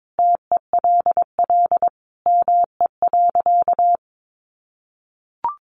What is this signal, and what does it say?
Morse code, heard as a keyed tone, about 22 wpm, 700 Hz. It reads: TELL ME.
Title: Sentences from Top 1500 Words 22wpm